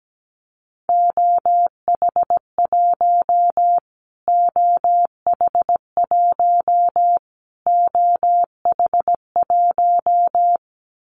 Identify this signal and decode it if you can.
Morse code, heard as a keyed tone, about 17 wpm, 700 Hz. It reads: OH1 OH1 OH1
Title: Callsign Prefixes for Top 25 DXCC - Repeated 3x - 17wpm